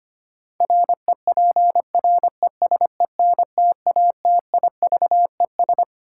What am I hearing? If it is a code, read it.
Morse code, heard as a keyed tone, about 25 wpm, 700 Hz. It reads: REPRESENTATIVES
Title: Sets of 5 Words Encore - New Word Drill - 25wpm